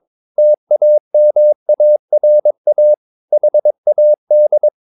Morse code, heard as a keyed tone, about 22 wpm, 600 Hz. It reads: TAMARA HAD
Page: CW med Gnister